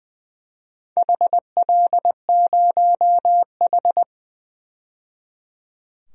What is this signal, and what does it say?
Morse code, heard as a keyed tone, about 20 wpm, 700 Hz. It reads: HL0H
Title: DXCC-100 Call Signs - 4 Characters - 20wpm